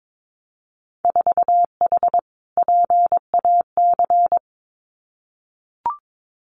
Morse code, heard as a keyed tone, about 22 wpm, 700 Hz. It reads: 4H PAC
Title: Winter Field Day Exchanges Encore 22wpm